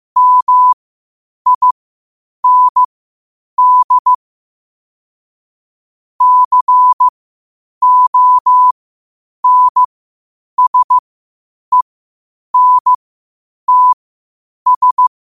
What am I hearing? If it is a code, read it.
Morse code, heard as a keyed tone, about 15 wpm, spaced as for 8 wpm, 1000 Hz. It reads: MIND CONSENTS
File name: qotdfiles / 8wpm